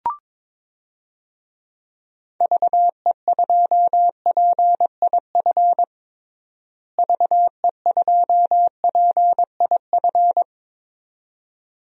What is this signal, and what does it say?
Morse code, heard as a keyed tone, about 22 wpm, 700 Hz. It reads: VE2PIF VE2PIF
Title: DXCC-100 Call Signs - 6 Characters - Repeated 2x - 22wpm